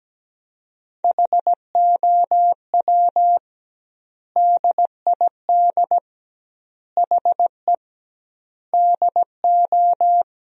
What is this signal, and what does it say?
Morse code, heard as a keyed tone, about 17 wpm, 700 Hz. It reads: HOW DID HE DO
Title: 2x Word-Spacing - Sentences from Top 100 Words - 17wpm